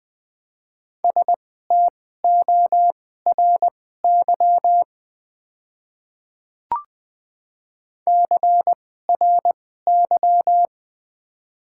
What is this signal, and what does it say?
Morse code, heard as a keyed tone, about 20 wpm, spaced as for 15 wpm, 700 Hz. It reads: STORY CRY
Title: 2x Character Spacing - Top 500 Words - 20wpm